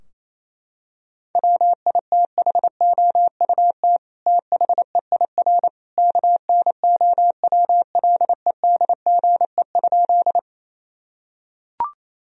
Morse code, heard as a keyed tone, about 28 wpm, 700 Hz. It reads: WITHOUT THEIR KNOWLEDGE?
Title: Sentences from Top 1000 Words - Review Part 3 - 28wpm